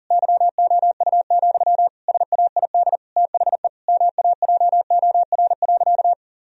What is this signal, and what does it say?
Morse code, heard as a keyed tone, about 37 wpm, 700 Hz. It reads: YOU, SAID THE MAJOR.
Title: Sentences from Top 1000 Words - Review Part 4 - 40wpm